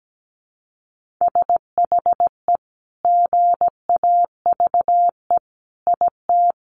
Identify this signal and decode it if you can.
Morse code, heard as a keyed tone, about 17 wpm, 700 Hz. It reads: SHE GAVE IT
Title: Sentences from Top 1500 Words 17wpm